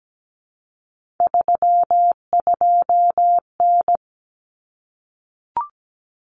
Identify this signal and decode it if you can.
Morse code, heard as a keyed tone, about 17 wpm, 700 Hz. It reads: 32N
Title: Common Cut Numbers - 3-Digit 17wpm